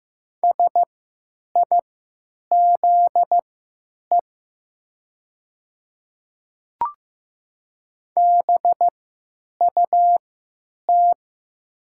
Morse code, heard as a keyed tone, about 15 wpm, spaced as for 9 wpm, 700 Hz. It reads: SIZE BUT
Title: 3x Character Spacing - Top 500 Words - 15wpm